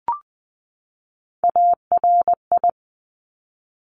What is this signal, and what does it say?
Morse code, heard as a keyed tone, about 20 wpm, 700 Hz. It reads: ARI
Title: Top Three Letter Combinations in Words 20wpm